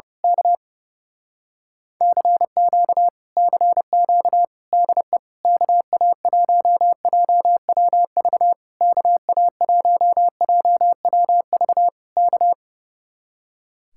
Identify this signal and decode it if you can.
Morse code, heard as a keyed tone, about 29 wpm, 700 Hz. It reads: K CQ CQ DE KA1JWV KA1JWV K
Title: Speed-Racing Common QSO Phrases Encore - 30wpm